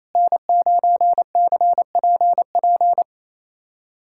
Morse code, heard as a keyed tone, about 28 wpm, 700 Hz. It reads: N9CPP
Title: USA Callsign DE Callsign - 5 Characters - 28wpm